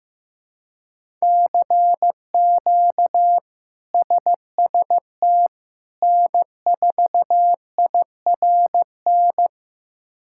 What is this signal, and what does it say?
Morse code, heard as a keyed tone, about 15 wpm, 700 Hz. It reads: CQ SST N4IRN
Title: K1USN SST QSO Practice - 15wpm